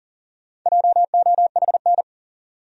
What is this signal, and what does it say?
Morse code, heard as a keyed tone, about 40 wpm, 700 Hz. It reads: JOHN
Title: USA Common Names - 40wpm